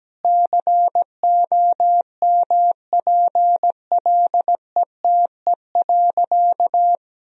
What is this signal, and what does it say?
Morse code, heard as a keyed tone, about 17 wpm, 700 Hz. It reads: COMPLETE.
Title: Sentences from Top 400 Words and Abbreviations 17wpm